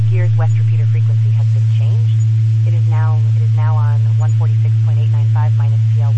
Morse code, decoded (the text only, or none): none